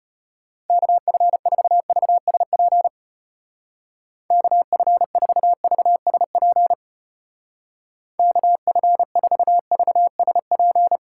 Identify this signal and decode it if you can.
Morse code, heard as a keyed tone, about 31 wpm, 700 Hz. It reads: KF4VSP KF4VSP KF4VSP
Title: Speed Racing - USA 2x3 Call Signs 25wpm